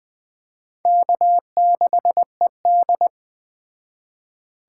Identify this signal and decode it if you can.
Morse code, heard as a keyed tone, about 20 wpm, 700 Hz. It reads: K6ED